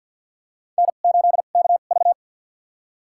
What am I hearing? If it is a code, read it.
Morse code, heard as a keyed tone, about 48 wpm, spaced as for 35 wpm, 700 Hz. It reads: N8KV
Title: USA Call Signs - 4 Characters - 2x Character Spacing - 50wpm